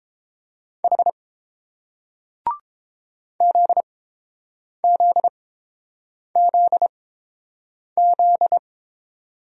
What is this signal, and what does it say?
Morse code, heard as a keyed tone, about 29 wpm, spaced as for 7 wpm, 690 Hz. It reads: H ZZZZ